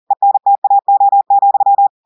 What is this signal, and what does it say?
Morse code, heard as a keyed tone, about 40 wpm, 800 Hz. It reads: ENTAO,
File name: audios / portugais / portugais texte en clair 2 vitesse 40 mots minutes